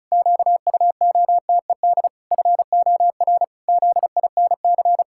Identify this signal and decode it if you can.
Morse code, heard as a keyed tone, about 34 wpm, 700 Hz. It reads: QUOTED FOR ZINC
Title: Pangram Challenge v2 - 35wpm